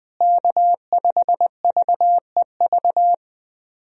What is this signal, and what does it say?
Morse code, heard as a keyed tone, about 20 wpm, 700 Hz. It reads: K5VEV
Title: USA Call Signs - 5 Characters - Repeated 3x - 20wpm